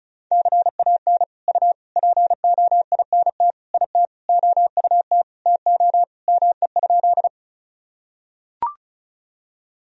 Morse code, heard as a keyed tone, about 35 wpm, 700 Hz. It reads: CAN U POINT IT OUT TO ME?
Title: Sentences from Top 200 Words and Abbreviations 35wpm